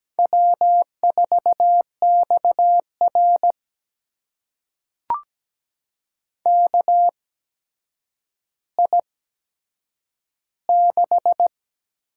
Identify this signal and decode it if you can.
Morse code, heard as a keyed tone, about 17 wpm, spaced as for 8 wpm, 700 Hz. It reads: W4XR KI6
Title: USA Call Signs - 8x Character Spacing - 17wpm